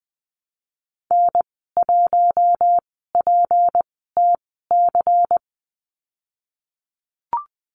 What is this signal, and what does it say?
Morse code, heard as a keyed tone, about 20 wpm, spaced as for 15 wpm, 700 Hz. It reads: N1PTC
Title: 173 Lesson 28g - Number 9 - Call Signs 20wpm